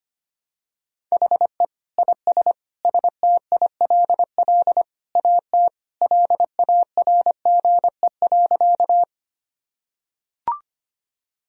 Morse code, heard as a keyed tone, about 25 wpm, 700 Hz. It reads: HE IS STILL AT LARGE.
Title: Sentences from Top 1000 Words - Review Part 1 - 25wpm